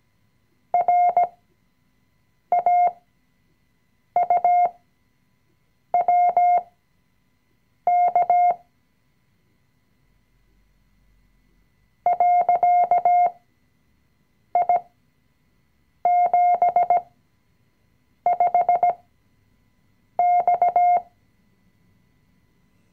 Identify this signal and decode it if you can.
Morse code, heard as a keyed tone, about 17 wpm, spaced as for 6 wpm, 710 Hz. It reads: RAUWK .I75X
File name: CW-Online-Kurs / Anfänger / Lektion 020